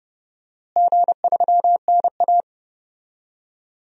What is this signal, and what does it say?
Morse code, heard as a keyed tone, about 30 wpm, 700 Hz. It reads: G3NA
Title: DXCC-100 Call Signs - 4 Characters - Repeated 2x - 30wpm